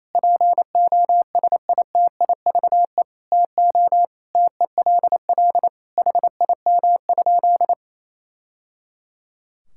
Morse code, heard as a keyed tone, about 27 wpm, 700 Hz. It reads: POSITIVE TO TELL HIM?
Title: Sentences from Top 1000 Words - Review Part 5 - 28wpm